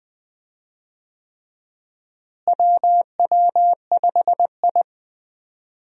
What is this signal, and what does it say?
Morse code, heard as a keyed tone, about 21 wpm, 700 Hz. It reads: WW5I
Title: USA Call Signs - 4 Characters - Repeated 3x - 20wpm